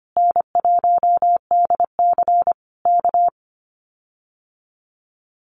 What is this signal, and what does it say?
Morse code, heard as a keyed tone, about 25 wpm, 700 Hz. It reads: N1DC K
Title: Common QSO Phrases Encore 25wpm